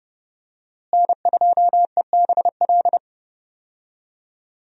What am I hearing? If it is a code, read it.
Morse code, heard as a keyed tone, about 30 wpm, 700 Hz. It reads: N2EBL
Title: USA Call Signs - 5 Characters - Repeated 3x - 30wpm